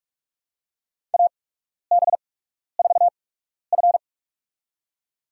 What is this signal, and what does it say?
Morse code, heard as a keyed tone, about 42 wpm, spaced as for 12 wpm, 700 Hz. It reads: AB4F
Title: USA Call Signs - 8x Character Spacing - 45wpm